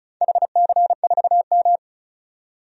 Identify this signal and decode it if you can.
Morse code, heard as a keyed tone, about 36 wpm, 700 Hz. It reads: HC4M
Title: DXCC-100 Call Signs - 4 Characters - 35wpm